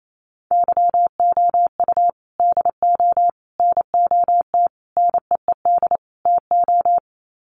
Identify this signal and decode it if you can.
Morse code, heard as a keyed tone, about 28 wpm, 700 Hz. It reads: YOU DO NOT NEED TO
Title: Sets of 5 Words 28wpm